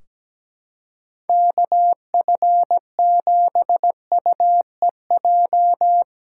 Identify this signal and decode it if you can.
Morse code, heard as a keyed tone, about 17 wpm, 700 Hz. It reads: KF7UEJ
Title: Practice QSO - Exchange 4 - Ending - 17wpm